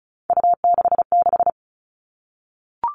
Morse code, heard as a keyed tone, about 34 wpm, 700 Hz. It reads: U66
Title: All Cut Numbers - 3-Digit 35wpm